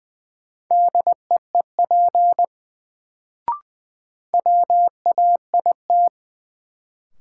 Morse code, heard as a keyed tone, about 20 wpm, 700 Hz. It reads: DEEP WAIT
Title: Sets of 5 Words Encore - New Word Drill - 20wpm